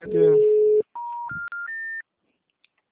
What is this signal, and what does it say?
Special information tone, with speech.